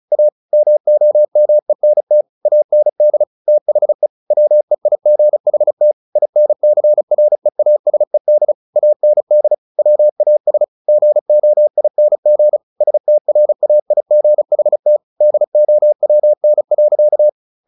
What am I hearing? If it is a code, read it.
Morse code, heard as a keyed tone, about 34 wpm, 600 Hz. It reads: A MOMENT AND THE WEIGHT INCREASED AND WAS GOING STRAIGHT DOWN.
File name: the old man and the sea / the old man and the sea 1 35wpm